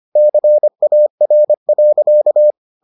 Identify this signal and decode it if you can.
Morse code, heard as a keyed tone, about 25 wpm, 600 Hz. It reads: CAR.